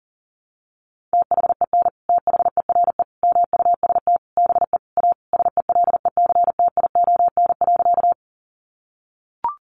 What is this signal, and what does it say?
Morse code, heard as a keyed tone, about 40 wpm, 700 Hz. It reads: THEN THERE MUST BE A SELECTION.
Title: Sentences from Top 1400 Words 40wpm